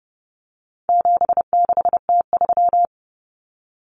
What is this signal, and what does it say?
Morse code, heard as a keyed tone, about 30 wpm, 700 Hz. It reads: 76T3
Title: Common Cut Numbers - 4-Digit 30wpm